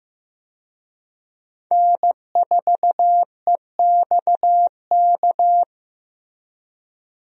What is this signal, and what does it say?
Morse code, heard as a keyed tone, about 15 wpm, 700 Hz. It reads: N4EXK
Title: USA Call Signs - 5 Characters - Repeated 2x - 15wpm